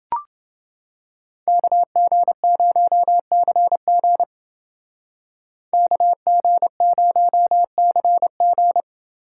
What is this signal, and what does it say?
Morse code, heard as a keyed tone, about 28 wpm, 700 Hz. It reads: KG0CG KG0CG